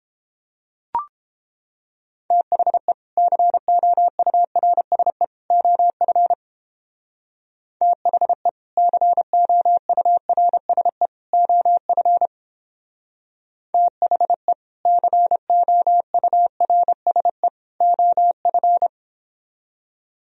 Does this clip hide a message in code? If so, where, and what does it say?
Yes: Morse code from 2.3 to 18.9 s, heard as a keyed tone, about 31 wpm, 700 Hz. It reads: THE COURSE OF THE COURSE OF THE COURSE OF